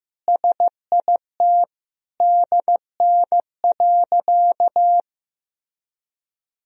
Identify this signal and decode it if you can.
Morse code, heard as a keyed tone, about 15 wpm, 700 Hz. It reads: SIT DN.
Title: Sentences from Top 500 Words and Abbreviations 15wpm